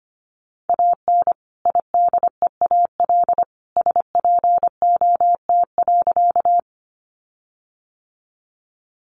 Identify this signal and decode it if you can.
Morse code, heard as a keyed tone, about 25 wpm, 700 Hz. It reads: AN IDEAL SPOT.
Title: Sentences from Top 1400 Words 25wpm